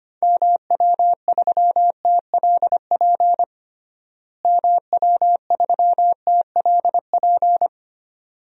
Morse code, heard as a keyed tone, about 25 wpm, 700 Hz. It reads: MW3TLP MW3TLP
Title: DXCC-100 Call Signs - 6 Characters - Repeated 2x - 25wpm